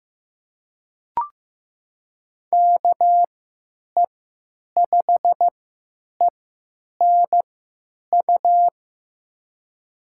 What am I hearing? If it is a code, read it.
Morse code, heard as a keyed tone, about 15 wpm, spaced as for 9 wpm, 700 Hz. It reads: KE5ENU
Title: USA Call Signs - 6 Characters - 3x Character Spacing - 15wpm